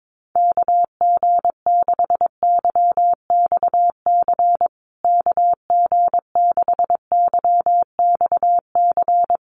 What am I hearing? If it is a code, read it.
Morse code, heard as a keyed tone, about 22 wpm, 700 Hz. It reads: KG6YXC KG6YXC
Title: USA Call Signs Repeated 2x 22wpm